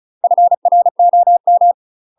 Morse code, heard as a keyed tone, about 36 wpm, 700 Hz. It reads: FROM
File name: UK MASTER QSOs TRAINING / 35WPM / UK MASTER QSOs 0038